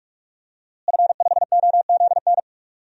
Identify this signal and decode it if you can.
Morse code, heard as a keyed tone, about 46 wpm, 700 Hz. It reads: F5OZN
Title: DXCC-100 Call Signs - 5 Characters - Repeated 3x - 45wpm